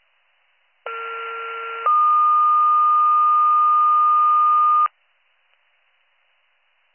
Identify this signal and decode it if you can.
Two-tone page, heard as a one-second tone, then a three-second tone, about 1370 then 1220 Hz. It is heard from 0.9 to 4.9 s.